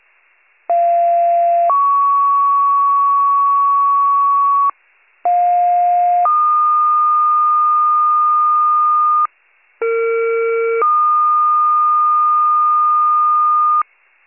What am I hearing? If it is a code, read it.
Two-tone page, heard as a one-second tone, then a three-second tone, about 690 then 1120 Hz, about 710 then 1220 Hz, about 460 then 1220 Hz. It is heard from 0.7 to 13.8 s.